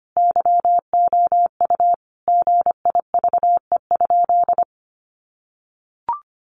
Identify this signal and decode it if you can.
Morse code, heard as a keyed tone, about 25 wpm, 700 Hz. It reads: YOU GIVE?
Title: Sentences from Top 200 Words 25wpm